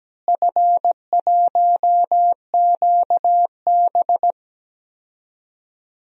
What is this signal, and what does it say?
Morse code, heard as a keyed tone, about 17 wpm, 700 Hz. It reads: F1QB
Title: DXCC-100 Call Signs - 4 Characters - Repeated 2x - 17wpm